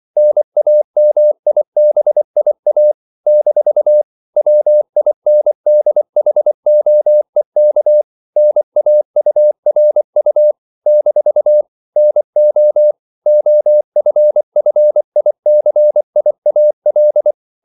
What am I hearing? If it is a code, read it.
Morse code, heard as a keyed tone, about 24 wpm, 600 Hz. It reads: NAMIBIA - WINDHOEK NAURU - NO OFFICIAL
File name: CW / Practice / NationalCapitals24